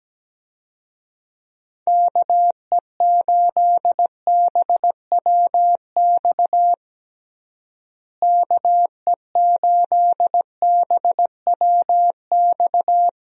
Morse code, heard as a keyed tone, about 17 wpm, 700 Hz. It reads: KE8BWX KE8BWX
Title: USA Call Signs - 6 Characters - Repeated 3x - 17wpm